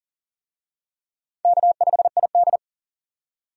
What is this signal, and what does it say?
Morse code, heard as a keyed tone, about 40 wpm, 700 Hz. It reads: K5ID
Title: USA Call Signs - 4 Characters - 40wpm